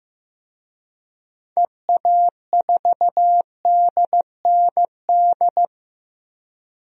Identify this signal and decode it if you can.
Morse code, heard as a keyed tone, about 15 wpm, 700 Hz. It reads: EA4DND
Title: DXCC-100 Call Signs - 6 Characters - 15wpm